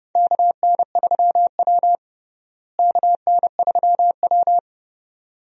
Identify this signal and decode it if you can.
Morse code, heard as a keyed tone, about 30 wpm, 700 Hz. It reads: KN3W KN3W
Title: USA Call Signs - 4 Characters - Repeated 2x - 30wpm